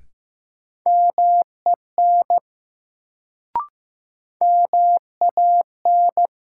Morse code, heard as a keyed tone, about 15 wpm, 700 Hz. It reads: MEN MAN
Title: Sentences from Top 200 Words - New Word Drill - 15wpm